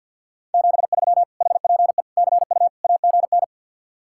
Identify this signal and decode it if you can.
Morse code, heard as a keyed tone, about 65 wpm, spaced as for 57 wpm, 700 Hz. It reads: 73 HPE CU AGN
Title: Speed-Racing Common QSO Phrases Encore - 50wpm